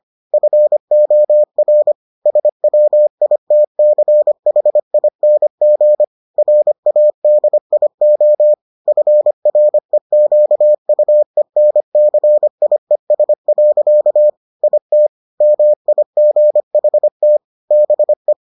Morse code, heard as a keyed tone, about 25 wpm, 600 Hz. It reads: FOR SWITCHING RADIO FREQUENCIES. IT MIGHT BE